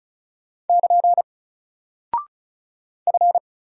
Morse code, heard as a keyed tone, about 36 wpm, 700 Hz. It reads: <KN> F
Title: Single Character (Letter:Number:Punctuation:Prosign) 35wpm